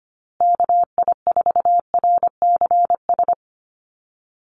Morse code, heard as a keyed tone, about 25 wpm, 700 Hz. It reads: KI4RCS
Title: USA Call Signs Repeated 2x 25wpm